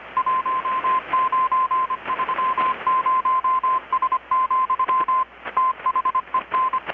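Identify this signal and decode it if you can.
Morse code, heard as a keyed tone, about 25 wpm, 1040 Hz, with radio noise. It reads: 1930S, THEN
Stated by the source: Morse from KPH's Night of Nights XX maritime radiotelegraphy event, received in Chiapas, Mexico about 3,800 km away on a Kenwood R-600 with a grounded 10 m long-wire antenna.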